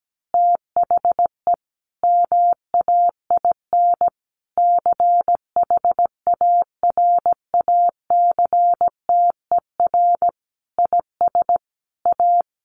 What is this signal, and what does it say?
Morse code, heard as a keyed tone, about 17 wpm, 700 Hz. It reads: THE MAIN CHARACTER IS A
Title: Additional Easy Phrases 17wpm